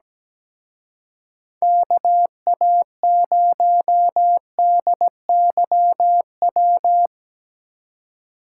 Morse code, heard as a keyed tone, about 18 wpm, 700 Hz. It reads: KA0DYW
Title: Speed Racing - USA Call Signs 17wpm